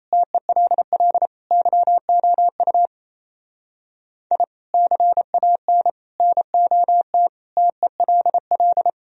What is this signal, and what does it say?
Morse code, heard as a keyed tone, about 28 wpm, 700 Hz. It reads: TELL YOU I CAN NOT TELL
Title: Speed-Racing - Sets of 5 Words 28wpm